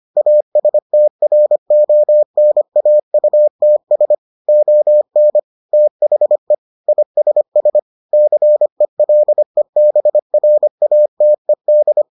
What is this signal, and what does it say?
Morse code, heard as a keyed tone, about 25 wpm, 600 Hz. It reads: ASTRONAUTS ON THE ISS CELEBRATED